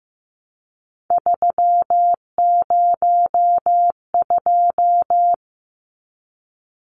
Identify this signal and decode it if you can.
Morse code, heard as a keyed tone, about 15 wpm, 700 Hz. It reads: 302
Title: Numbers - 3-Digit 15wpm